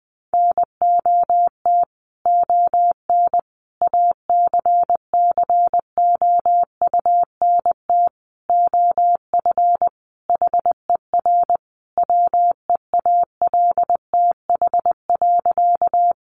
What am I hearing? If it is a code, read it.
Morse code, heard as a keyed tone, about 20 wpm, 700 Hz. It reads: NOT ON ACCOUNT OF HER WEALTH.
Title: Sentences from Top 1200 Words 20wpm